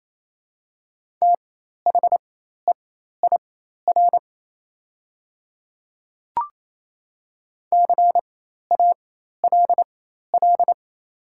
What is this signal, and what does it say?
Morse code, heard as a keyed tone, about 28 wpm, spaced as for 13 wpm, 700 Hz. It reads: THEIR CALL
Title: 4x Character Spacing - Top 300 Words - 28wpm